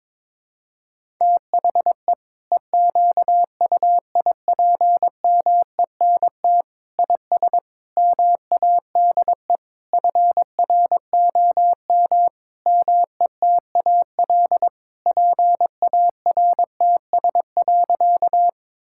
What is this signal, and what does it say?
Morse code, heard as a keyed tone, about 21 wpm, 700 Hz. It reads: THE EQUIPMENT IS MADE FROM METAL PARTS.
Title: Sentences from Top 1000 Words - Review Part 6 - 22wpm